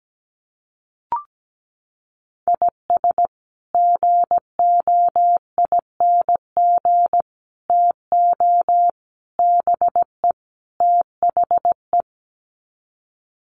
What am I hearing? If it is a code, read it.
Morse code, heard as a keyed tone, about 17 wpm, 700 Hz. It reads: IS GOING TO BE THE
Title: Sets of 5 Words 17wpm